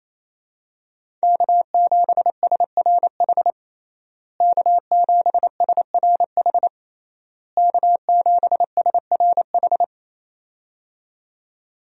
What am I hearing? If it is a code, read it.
Morse code, heard as a keyed tone, about 27 wpm, 700 Hz. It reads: K7SRH K7SRH K7SRH
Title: USA Call Signs - 5 Characters - Repeated 3x - 28wpm